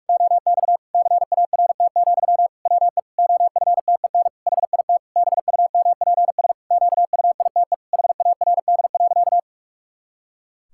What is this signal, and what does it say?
Morse code, heard as a keyed tone, about 41 wpm, 700 Hz. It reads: OX CART, WE OFTEN HIT BUMPS QUITE HARD.
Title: Pangram Challenge v2 - 45wpm